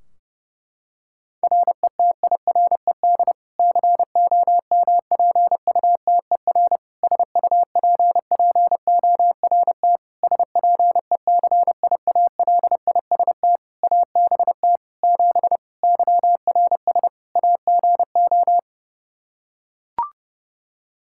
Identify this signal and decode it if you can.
Morse code, heard as a keyed tone, about 30 wpm, 700 Hz. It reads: RETIRED COMPUTER SUPPORT SPECIALIST ABT 7 YRS AGO
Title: Practice QSO - Exchange 3 - More Info - 30wpm